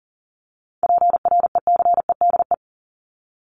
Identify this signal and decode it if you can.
Morse code, heard as a keyed tone, about 40 wpm, 700 Hz. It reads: PRECEDE
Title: ING Suffix - Top 100 Words with Base Words 40wpm